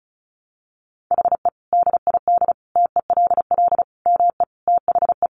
Morse code, heard as a keyed tone, about 34 wpm, 700 Hz. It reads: HE DID TELL ME THE
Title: Sentences from Top 1000 Words 35wpm